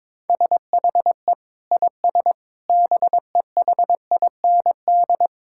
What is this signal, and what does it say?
Morse code, heard as a keyed tone, about 21 wpm, 700 Hz. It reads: SHE IS BEHIND
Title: Sentences from Top 1000 Words - Review Part 6 - 22wpm